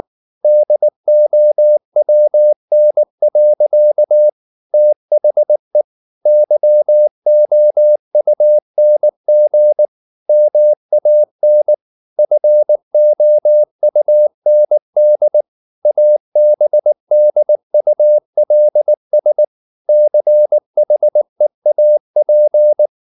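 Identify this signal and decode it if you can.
Morse code, heard as a keyed tone, about 19 wpm, 600 Hz. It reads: DOWN. THE YOUNG MAN FOUND ABDULS CHEAP
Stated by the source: Morse code at 19 wpm from CW med Gnister, reading Ken Follett's novel Never in English.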